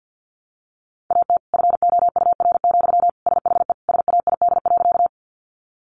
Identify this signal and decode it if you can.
Morse code, heard as a keyed tone, about 50 wpm, 700 Hz. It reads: AT FOUR, SHE SAID.
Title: Sentences from Top 300 Words 50wpm